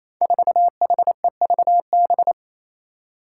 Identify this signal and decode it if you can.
Morse code, heard as a keyed tone, about 28 wpm, 700 Hz. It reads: 4HEVB
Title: Five Challenging Letter-Number Combinations 28wpm